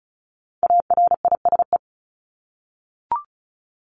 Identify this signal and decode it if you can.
Morse code, heard as a keyed tone, about 34 wpm, 700 Hz. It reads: ARISE
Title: Top 1501-1750 Words 35wpm